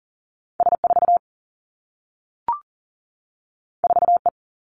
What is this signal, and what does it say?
Morse code, heard as a keyed tone, about 40 wpm, 700 Hz. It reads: S 4 4 E
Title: Two Challenging Letter-Number Combinations 40wpm